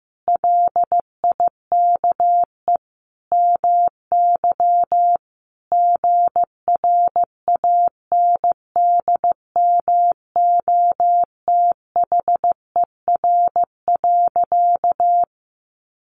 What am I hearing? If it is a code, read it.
Morse code, heard as a keyed tone, about 15 wpm, 700 Hz. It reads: LIKE MY GRANDMOTHER.
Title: Sentences from Top 1200 Words 15wpm